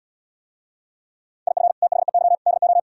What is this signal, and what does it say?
Morse code, heard as a keyed tone, about 77 wpm, 700 Hz. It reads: IF THEY DO